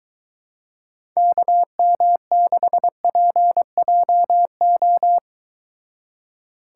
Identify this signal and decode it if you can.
Morse code, heard as a keyed tone, about 22 wpm, 700 Hz. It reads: KM6PJO